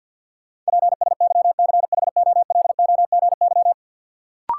Morse code, heard as a keyed tone, about 51 wpm, 700 Hz. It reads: PSYCHOLOGY
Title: Sentences from Top 1400 Words - New Word Drill Alt - 50wpm